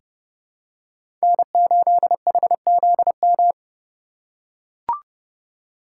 Morse code, heard as a keyed tone, about 32 wpm, 700 Hz. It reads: N8HZM
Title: K1USN SST QSO Practice - 30wpm